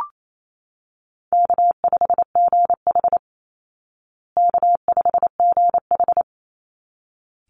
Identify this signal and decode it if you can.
Morse code, heard as a keyed tone, about 28 wpm, 700 Hz. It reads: K5GH K5GH
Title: Easy Call Signs Repeated 2x - 28wpm